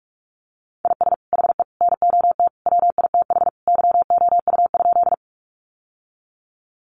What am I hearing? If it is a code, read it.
Morse code, heard as a keyed tone, about 44 wpm, 700 Hz. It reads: IS HE NOT WITH YOU?
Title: Sentences from Top 600 Words 45wpm